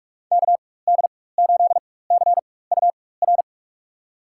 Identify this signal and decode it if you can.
Morse code, heard as a keyed tone, about 46 wpm, spaced as for 21 wpm, 700 Hz. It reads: KD8CUR
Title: USA Call Signs - 6 Characters - 4x Character Spacing - 45wpm